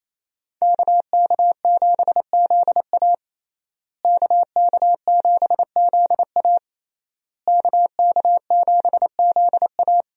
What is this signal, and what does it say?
Morse code, heard as a keyed tone, about 27 wpm, 700 Hz. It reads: KK7ZA KK7ZA KK7ZA
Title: USA Call Signs - 5 Characters - Repeated 3x - 28wpm